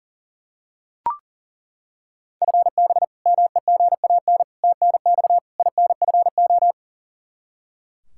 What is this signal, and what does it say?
Morse code, heard as a keyed tone, about 40 wpm, 700 Hz. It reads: FB MEGAN TNX INFO
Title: Practice QSO - Exchange 3 - More Info - 40wpm